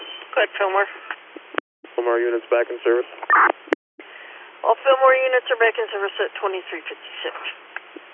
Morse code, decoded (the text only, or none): none